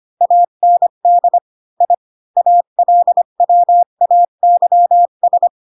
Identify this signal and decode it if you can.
Morse code, heard as a keyed tone, about 25 wpm, 700 Hz. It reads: AND I ALWAYS
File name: Pinocchio / Pinocchio ch8 20 25 ik1hge